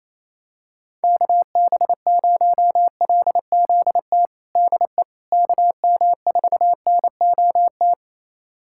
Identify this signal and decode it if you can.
Morse code, heard as a keyed tone, about 28 wpm, 700 Hz. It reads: KB0LZT DE KM4NOT
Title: USA Callsign DE Callsign - 6 Characters - 28wpm